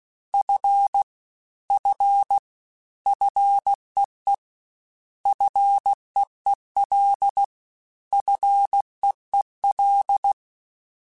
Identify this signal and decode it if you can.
Morse code, heard as a keyed tone, about 16 wpm, 800 Hz. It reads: F F FEE FEEL FEEL
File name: Combinations / Combinations-mp3s / Combinations@15